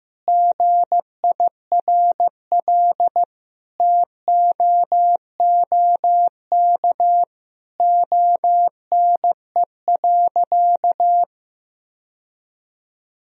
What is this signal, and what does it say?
Morse code, heard as a keyed tone, about 16 wpm, 700 Hz. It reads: GIRL TOOK ONE.
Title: Sentences from Top 1000 Words - Review Part 1 - 15wpm